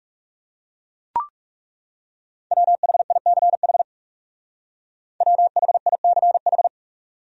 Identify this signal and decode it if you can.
Morse code, heard as a keyed tone, about 38 wpm, 700 Hz. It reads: WHICH WHICH